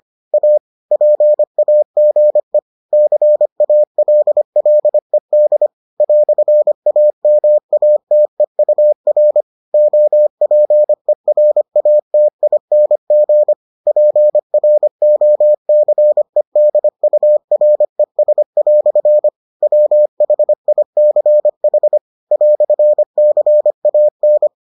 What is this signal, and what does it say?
Morse code, heard as a keyed tone, about 25 wpm, 600 Hz. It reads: A PAGE CALLED "AMATEUR OPERATING PROCEDURES" WHICH "CAN